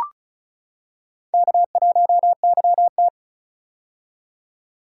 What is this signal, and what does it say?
Morse code, heard as a keyed tone, about 36 wpm, 700 Hz. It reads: K1YT
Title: K1USN SST QSO Practice - 35wpm